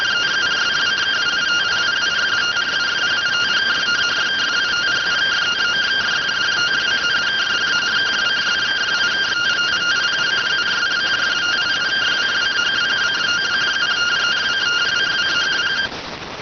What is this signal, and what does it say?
Two-tone teleprinter signal, about 170 Hz shift.